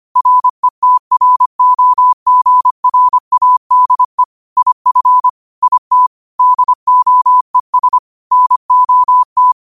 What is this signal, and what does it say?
Morse code, heard as a keyed tone, about 25 wpm, 1000 Hz. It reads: RETROGRADE IF IT DOES NOT
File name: qotdfiles / 25wpm